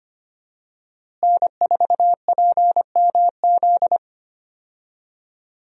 Morse code, heard as a keyed tone, about 25 wpm, 700 Hz. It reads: N4PMZ
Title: USA Call Signs - 5 Characters - Repeated 3x - 25wpm